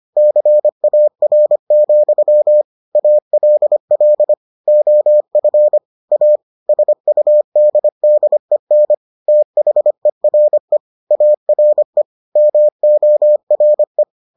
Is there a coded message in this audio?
Yes: Morse code from 0.2 to 14.0 s, heard as a keyed tone, about 25 wpm, 600 Hz. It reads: CAR, ALL OF A SUDDEN THERE ARE MORE